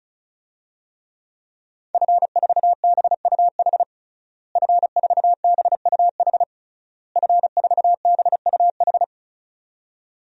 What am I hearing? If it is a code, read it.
Morse code, heard as a keyed tone, about 36 wpm, 700 Hz. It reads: F4BUH F4BUH F4BUH
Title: DXCC-100 Call Signs - 5 Characters - Repeated 3x - 35wpm